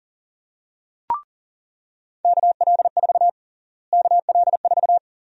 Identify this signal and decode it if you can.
Morse code, heard as a keyed tone, about 40 wpm, 700 Hz. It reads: KL4 KL4
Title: USA Call Sign Prefixes - Repeated 2x - 40wpm